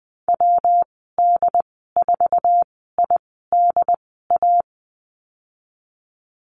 Morse code, heard as a keyed tone, about 20 wpm, spaced as for 15 wpm, 700 Hz. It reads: WD4IDA
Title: 122 Lesson 20g - Number 3 - Call Signs 20wpm